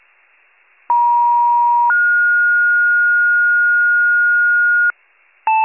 Two-tone page, heard as a one-second tone, then a three-second tone, about 950 then 1430 Hz, with noise. It is heard from 0.9 to 4.9 s.